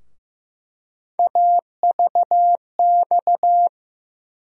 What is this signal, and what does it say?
Morse code, heard as a keyed tone, about 15 wpm, 700 Hz. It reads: AVX
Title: USA Call Sign Postfixes - 3 Letters 15wpm